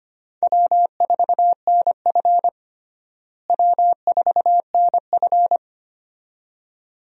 Morse code, heard as a keyed tone, about 25 wpm, 700 Hz. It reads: W4NF W4NF
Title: USA Call Signs - 4 Characters - Repeated 2x - 25wpm